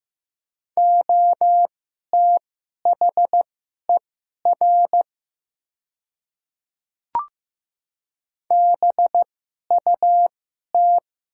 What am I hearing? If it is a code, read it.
Morse code, heard as a keyed tone, about 15 wpm, spaced as for 11 wpm, 700 Hz. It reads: OTHER BUT
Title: 2x Character Spacing - Top 100 Words - 15wpm